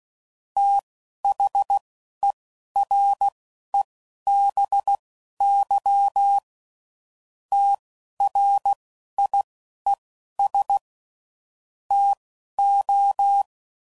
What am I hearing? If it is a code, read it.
Morse code, heard as a keyed tone, about 16 wpm, spaced as for 11 wpm, 800 Hz. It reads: THEREBY TRIES TO
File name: CW Code Groups @10WPM / Technical Subjects / XC